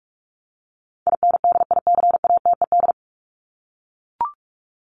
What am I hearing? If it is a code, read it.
Morse code, heard as a keyed tone, about 44 wpm, 700 Hz. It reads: INDICATED
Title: ED Suffix - Top 100 Words with Base Words 45wpm